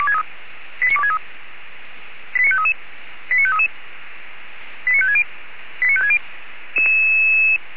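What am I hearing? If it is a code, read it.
Five-tone selective call (ZVEI), heard as quick runs of five tones, tones of about 70 ms. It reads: 89263 89533 89533 89560 89560
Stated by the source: ZVEI five-tone call 89461 received by SDR.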